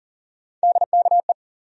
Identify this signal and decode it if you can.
Morse code, heard as a keyed tone, about 40 wpm, 700 Hz. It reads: DKE
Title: USA Call Sign Postfixes - 3 Letters 40wpm